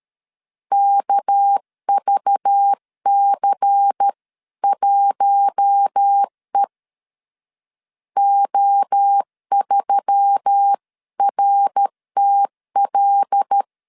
Morse code, heard as a keyed tone, about 13 wpm, 800 Hz. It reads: KVC1E O3RTL
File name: CW / DET / 1 6 1500C S C 12wpm 3533